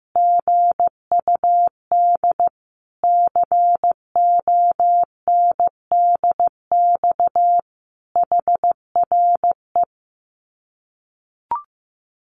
Morse code, heard as a keyed tone, about 15 wpm, 700 Hz. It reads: GUD CONDX HRE
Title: Common QSO Phrases 15wpm